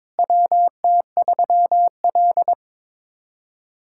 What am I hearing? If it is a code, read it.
Morse code, heard as a keyed tone, about 21 wpm, 700 Hz. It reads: WT3L
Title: USA Callsign DE Callsign - 4 Characters - 22wpm